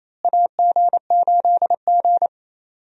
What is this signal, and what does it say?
Morse code, heard as a keyed tone, about 28 wpm, 700 Hz. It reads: AG8G
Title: USA Call Signs - 4 Characters - Repeated 3x - 28wpm